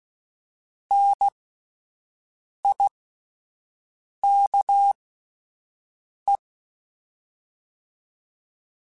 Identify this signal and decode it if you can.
Morse code, heard as a keyed tone, about 16 wpm, spaced as for 5 wpm, 800 Hz. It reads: NIKE